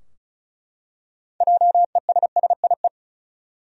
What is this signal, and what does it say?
Morse code, heard as a keyed tone, about 34 wpm, 700 Hz. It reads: JESSIE
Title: USA Common Names - 35wpm